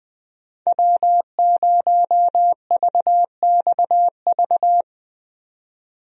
Morse code, heard as a keyed tone, about 20 wpm, 700 Hz. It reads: W0VXV
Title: USA Call Signs - 5 Characters - 20wpm